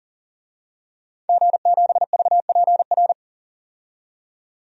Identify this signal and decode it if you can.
Morse code, heard as a keyed tone, about 40 wpm, 700 Hz. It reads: G7VPR
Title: DXCC-100 Call Signs - 5 Characters - 40wpm